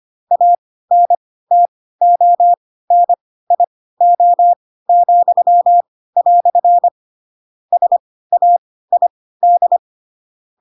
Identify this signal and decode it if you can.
Morse code, heard as a keyed tone, about 25 wpm, spaced as for 16 wpm, 700 Hz. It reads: ANTONIO," SAID
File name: Pinocchio / Pinocchio ch2 10 25 ik1hge